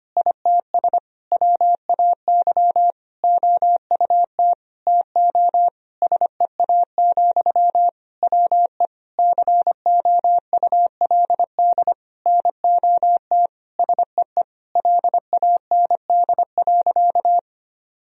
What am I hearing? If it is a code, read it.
Morse code, heard as a keyed tone, about 25 wpm, 700 Hz. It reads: ITS WAY OUT TO SEA, WE COULD NOT SEE LAND.
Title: Sentences from Top 1000 Words - Review Part 2 - 25wpm